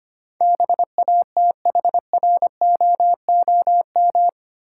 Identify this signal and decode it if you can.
Morse code, heard as a keyed tone, about 25 wpm, 700 Hz. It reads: BATHROOM
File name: Eight Letter Words - 25wpm